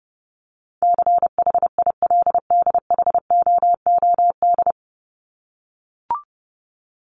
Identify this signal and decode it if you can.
Morse code, heard as a keyed tone, about 30 wpm, 700 Hz. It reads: CHILDHOOD
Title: Top 1751-2000 Words 30wpm